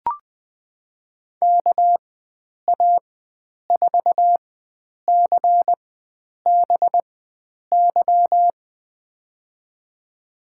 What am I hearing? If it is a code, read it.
Morse code, heard as a keyed tone, about 21 wpm, spaced as for 9 wpm, 700 Hz. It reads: KA4CBY
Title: USA Call Signs - 6 Characters - 4x Character Spacing - 20wpm